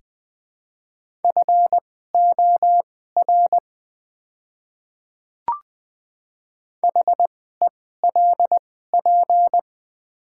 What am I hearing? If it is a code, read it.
Morse code, heard as a keyed tone, about 20 wpm, spaced as for 14 wpm, 700 Hz. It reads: FOR HELP
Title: 2x Character Spacing - Top 200 Words - 20wpm